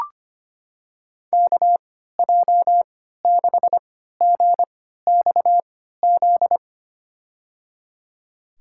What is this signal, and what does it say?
Morse code, heard as a keyed tone, about 25 wpm, spaced as for 14 wpm, 700 Hz. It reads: KJ6GXZ
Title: USA Call Signs - 6 Characters - 3x Character Spacing - 25wpm